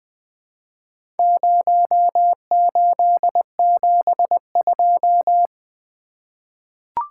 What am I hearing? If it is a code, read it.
Morse code, heard as a keyed tone, about 20 wpm, 700 Hz. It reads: 0872
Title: Four Letter-Number Combinations - 20wpm